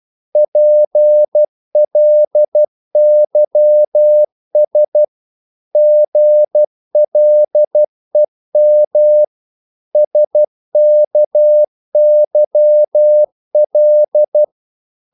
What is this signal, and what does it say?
Morse code, heard as a keyed tone, about 12 wpm, 600 Hz. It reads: PLYS GLEM SKYL
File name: Korte ord DK 12wpm